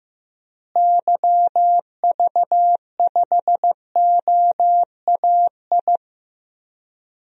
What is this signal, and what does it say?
Morse code, heard as a keyed tone, about 16 wpm, 700 Hz. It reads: YV5OAI
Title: DXCC-100 Call Signs - 6 Characters - Repeated 3x - 15wpm